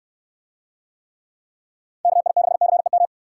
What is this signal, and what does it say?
Morse code, heard as a keyed tone, about 69 wpm, 700 Hz. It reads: KI6CEM